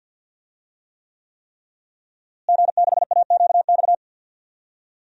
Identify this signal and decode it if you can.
Morse code, heard as a keyed tone, about 49 wpm, 700 Hz. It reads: G6AQX
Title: DXCC-100 Call Signs - 5 Characters - Repeated 3x - 50wpm